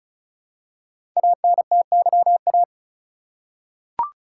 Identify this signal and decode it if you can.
Morse code, heard as a keyed tone, about 36 wpm, 690 Hz. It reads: ANTYA
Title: Code Groups - 5 Letter:Numbers - 35wpm